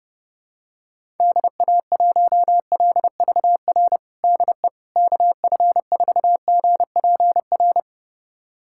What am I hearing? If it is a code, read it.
Morse code, heard as a keyed tone, about 29 wpm, 700 Hz. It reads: DA1LVR DE KF4GPR
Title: USA Callsign DE Callsign - 6 Characters - 30wpm